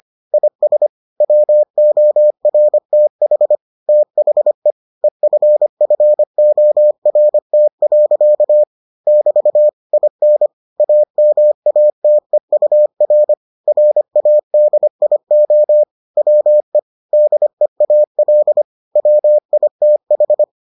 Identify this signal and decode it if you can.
Morse code, heard as a keyed tone, about 25 wpm, 600 Hz. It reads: IS WORTH THE EFFORT. <BT> IN AMATEUR RADIO WE DEAL WITH